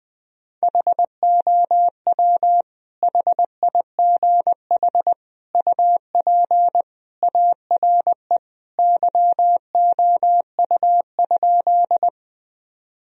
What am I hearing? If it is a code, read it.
Morse code, heard as a keyed tone, about 20 wpm, 700 Hz. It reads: HOW HIGH UP ARE YOU?
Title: Sentences from Top 1000 Words - Review Part 1 - 20wpm